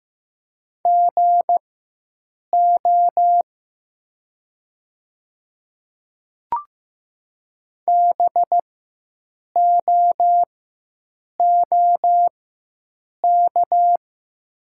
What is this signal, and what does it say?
Morse code, heard as a keyed tone, about 15 wpm, spaced as for 7 wpm, 700 Hz. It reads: GO BOOK
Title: 4x Character Spacing - Top 300 Words - 15wpm